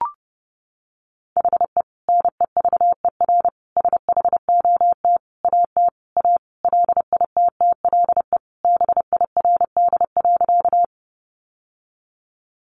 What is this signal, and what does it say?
Morse code, heard as a keyed tone, about 30 wpm, 700 Hz. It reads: HE NEVER SHOT AT A LITTLE BIRD.
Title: Sentences from Top 1100 Words 30wpm